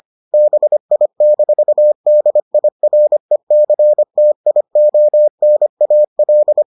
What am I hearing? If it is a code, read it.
Morse code, heard as a keyed tone, about 25 wpm, 600 Hz. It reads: BI-DIRECTIONAL